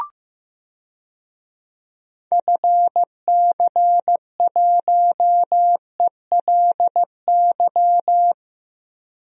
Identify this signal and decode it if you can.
Morse code, heard as a keyed tone, about 15 wpm, 700 Hz. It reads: FC1ELY
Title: DXCC-100 Call Signs - 6 Characters - Repeated 2x - 15wpm